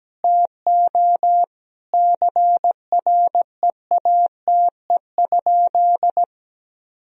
Morse code, heard as a keyed tone, about 17 wpm, 700 Hz. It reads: TO CREATE?
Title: Sentences from Top 1000 Words - Review Part 3 - 17wpm